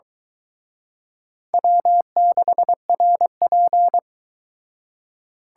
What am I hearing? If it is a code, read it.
Morse code, heard as a keyed tone, about 22 wpm, 700 Hz. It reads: W6RP